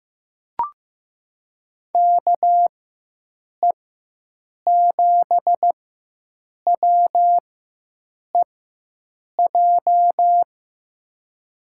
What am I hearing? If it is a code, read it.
Morse code, heard as a keyed tone, about 15 wpm, spaced as for 7 wpm, 700 Hz. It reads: KE7WEJ
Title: USA Call Signs - 6 Characters - 4x Character Spacing - 15wpm